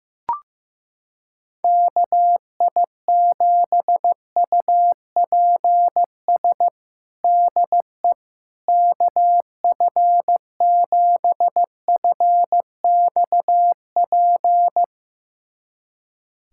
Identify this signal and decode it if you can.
Morse code, heard as a keyed tone, about 15 wpm, 700 Hz. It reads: KI7UPS DE KF7FXP
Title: USA Callsign DE Callsign - 6 Characters - 15wpm